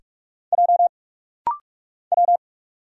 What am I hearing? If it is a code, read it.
Morse code, heard as a keyed tone, about 46 wpm, 700 Hz. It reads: J W
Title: Single Letters - Rapid-Fire 45wpm